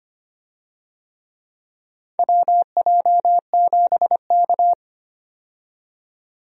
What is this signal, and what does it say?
Morse code, heard as a keyed tone, about 25 wpm, 700 Hz. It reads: WJ7K
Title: USA Call Signs - 4 Characters - 25wpm